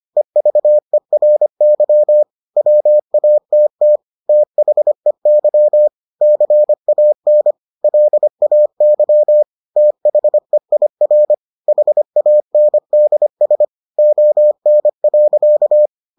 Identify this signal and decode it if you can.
Morse code, heard as a keyed tone, about 25 wpm, 600 Hz. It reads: EVERY WATT THEY CAN LAY THEIR HANDS ON.